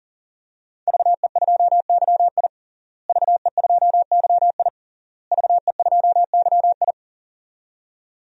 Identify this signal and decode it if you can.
Morse code, heard as a keyed tone, about 40 wpm, 700 Hz. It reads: VE2YI VE2YI VE2YI
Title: DXCC-100 Call Signs - 5 Characters - Repeated 3x - 40wpm